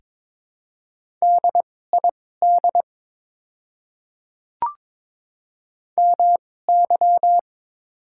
Morse code, heard as a keyed tone, about 22 wpm, spaced as for 16 wpm, 690 Hz. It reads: DID MY
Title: 2x Character Spacing - Top 100 Words - 22wpm